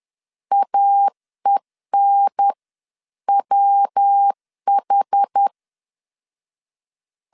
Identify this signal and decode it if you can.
Morse code, heard as a keyed tone, about 10 wpm, spaced as for 9 wpm, 800 Hz. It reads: AENWH
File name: 10wpm